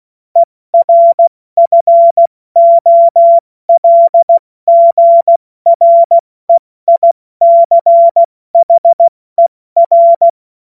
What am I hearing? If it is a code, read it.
Morse code, heard as a keyed tone, about 16 wpm, 680 Hz. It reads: ERFOLGREICHER
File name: Morsekurs / Witze / Witz 02. August 2022 70 F80